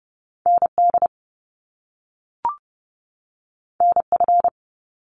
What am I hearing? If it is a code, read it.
Morse code, heard as a keyed tone, about 30 wpm, 700 Hz. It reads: ND NF
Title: Top Two Letter Combinations in Words 30wpm